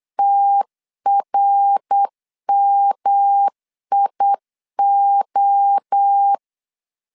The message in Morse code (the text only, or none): TRMIO